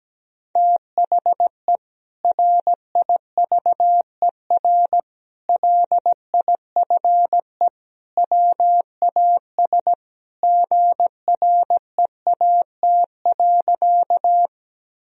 Morse code, heard as a keyed tone, about 17 wpm, 700 Hz. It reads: THE RIVER LIFE WAS GREAT.
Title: Sentences from Top 1000 Words - Review Part 1 - 17wpm